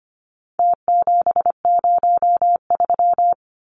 Morse code, heard as a keyed tone, about 25 wpm, 700 Hz. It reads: T703
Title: Numbers - Contest Style - 4-Digit 25wpm